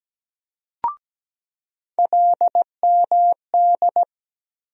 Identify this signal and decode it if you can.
Morse code, heard as a keyed tone, about 17 wpm, 700 Hz. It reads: LMD
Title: USA Call Sign Postfixes - 3 Letters 17wpm